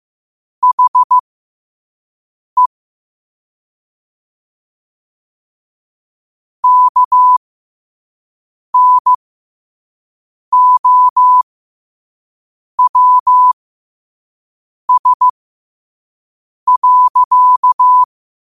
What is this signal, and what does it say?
Morse code, heard as a keyed tone, about 15 wpm, spaced as for 5 wpm, 1000 Hz. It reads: HE KNOWS.
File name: qotdfiles / 5wpm